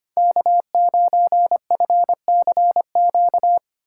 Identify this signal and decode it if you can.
Morse code, heard as a keyed tone, about 25 wpm, 700 Hz. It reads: K9FCQ
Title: USA Call Signs - 5 Characters - 2x Character Spacing - 25wpm